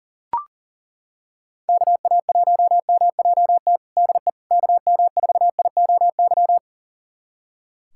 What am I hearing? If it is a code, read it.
Morse code, heard as a keyed tone, about 40 wpm, 700 Hz. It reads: KA1MJT DE KM4IOY
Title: USA Callsign DE Callsign - 6 Characters - 40wpm